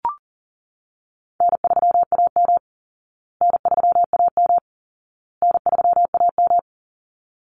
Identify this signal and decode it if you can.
Morse code, heard as a keyed tone, about 40 wpm, 700 Hz. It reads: N3AM N3AM N3AM
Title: Easy Call Signs Repeated 3x - 40wpm